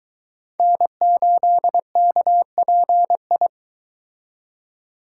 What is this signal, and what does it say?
Morse code, heard as a keyed tone, about 22 wpm, 700 Hz. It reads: N8KPI